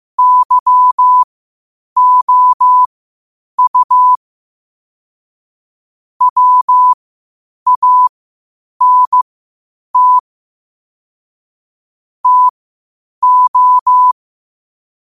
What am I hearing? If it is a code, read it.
Morse code, heard as a keyed tone, about 15 wpm, spaced as for 8 wpm, 1000 Hz. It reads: YOU WANT TO
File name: 8wpm